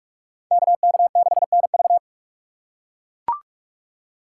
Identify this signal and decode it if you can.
Morse code, heard as a keyed tone, about 42 wpm, 700 Hz. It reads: KK6NV